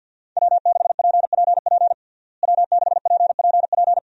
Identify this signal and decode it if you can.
Morse code, heard as a keyed tone, about 49 wpm, 700 Hz. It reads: W6PPP W6PPP
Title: USA Call Signs - 5 Characters - Repeated 2x - 50wpm